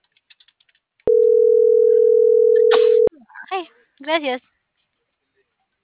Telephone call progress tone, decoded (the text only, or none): ringback tone